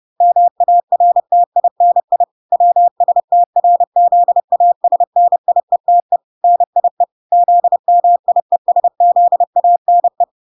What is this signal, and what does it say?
Morse code, heard as a keyed tone, about 30 wpm, 700 Hz. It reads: MARTINI WSTRZASNIETE NIE ZMIESZANE
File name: 30wpm